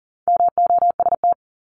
Morse code, heard as a keyed tone, about 40 wpm, 700 Hz. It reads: MOST
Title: LY Suffix - Top 100 Words with Base Words 40wpm